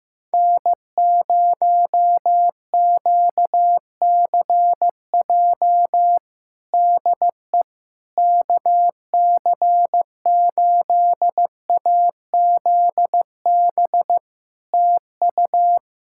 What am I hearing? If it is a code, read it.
Morse code, heard as a keyed tone, about 15 wpm, 700 Hz. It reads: N0QCJ DE KC8AZB TU
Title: Practice QSO - Exchange 4 - Ending - 15wpm